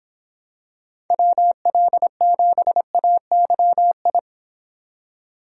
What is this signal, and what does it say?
Morse code, heard as a keyed tone, about 26 wpm, 700 Hz. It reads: WL7AYI